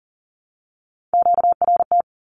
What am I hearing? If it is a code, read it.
Morse code, heard as a keyed tone, about 40 wpm, 700 Hz. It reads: QRT
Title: Q-Codes Repeated 3x 40wpm